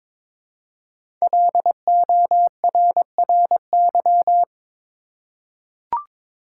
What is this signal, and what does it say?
Morse code, heard as a keyed tone, about 22 wpm, 700 Hz. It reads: LORRY
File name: 5 Letter Words - First 3 Letters Given - 22wpm